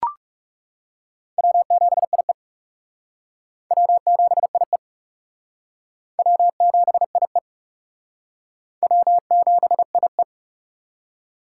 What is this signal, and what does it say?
Morse code, heard as a keyed tone, about 39 wpm, 700 Hz. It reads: W7IE W7IE W7IE W7IE